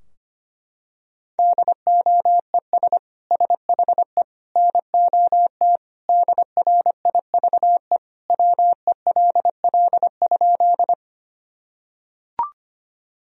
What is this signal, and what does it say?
Morse code, heard as a keyed tone, about 25 wpm, 700 Hz. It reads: DOES SHE NOT DRIVE WELL?
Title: Sentences from Top 1000 Words - Review Part 2 - 25wpm